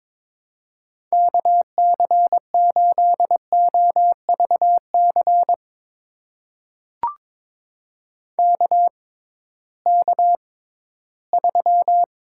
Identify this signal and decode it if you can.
Morse code, heard as a keyed tone, about 23 wpm, spaced as for 14 wpm, 700 Hz. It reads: KC8OVC KK3
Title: USA Call Signs - 6x Character Spacing - 22wpm